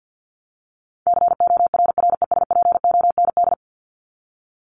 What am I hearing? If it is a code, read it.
Morse code, heard as a keyed tone, about 49 wpm, 700 Hz. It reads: CORRESPOND
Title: ING Suffix - Top 100 Words with Base Words 50wpm